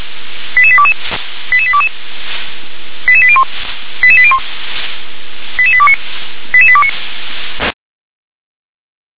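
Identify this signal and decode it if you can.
Five-tone selective call (ZVEI), heard as quick runs of five tones, tones of about 70 ms. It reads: 88822 88822 88801 88801 88829 88829
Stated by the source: ZVEI five-tone selective call received on SDR.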